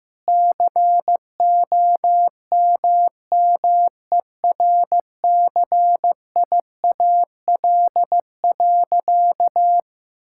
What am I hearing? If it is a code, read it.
Morse code, heard as a keyed tone, about 15 wpm, 700 Hz. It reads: COMMERCIAL.
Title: Sentences from Top 1000 Words - Review Part 5 - 15wpm